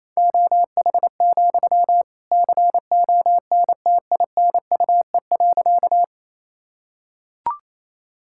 Morse code, heard as a keyed tone, about 28 wpm, 700 Hz. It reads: OH, CONTINUE.
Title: Sentences from Top 1000 Words - Review Part 6 - 28wpm